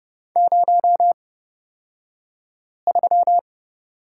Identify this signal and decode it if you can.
Morse code, heard as a keyed tone, about 29 wpm, spaced as for 5 wpm, 690 Hz. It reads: 03
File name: Lesson 36d - BT - Two Characters 30wpm